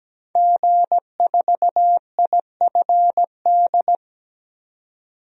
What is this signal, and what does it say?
Morse code, heard as a keyed tone, about 17 wpm, 700 Hz. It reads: G4IFD
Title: DXCC-100 Call Signs - 5 Characters - 17wpm